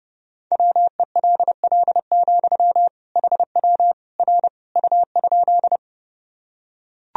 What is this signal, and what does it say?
Morse code, heard as a keyed tone, about 30 wpm, 700 Hz. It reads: WELL, HW R U?
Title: Sentences from Top 200 Words and Abbreviations 30wpm